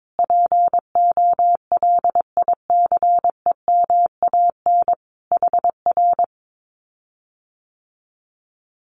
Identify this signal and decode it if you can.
Morse code, heard as a keyed tone, about 22 wpm, 700 Hz. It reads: POLICEMAN HR
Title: Common QSO Phrases Encore 22wpm